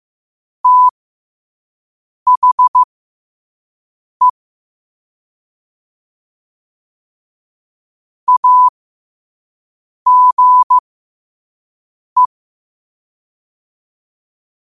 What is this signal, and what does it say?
Morse code, heard as a keyed tone, about 15 wpm, spaced as for 5 wpm, 1000 Hz. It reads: THE AGE
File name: qotdfiles / 5wpm